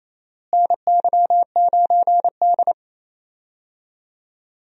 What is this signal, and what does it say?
Morse code, heard as a keyed tone, about 28 wpm, 700 Hz. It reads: NY9D
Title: USA Call Signs - 4 Characters - Repeated 2x - 28wpm